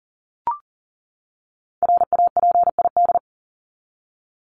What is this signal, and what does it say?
Morse code, heard as a keyed tone, about 40 wpm, 700 Hz. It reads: RAPID
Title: LY Suffix - Top 100 Words with Base Words 40wpm